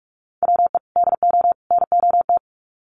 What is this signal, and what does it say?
Morse code, heard as a keyed tone, about 44 wpm, 700 Hz. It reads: WE DO NOT
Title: Sets of 3 Words 45wpm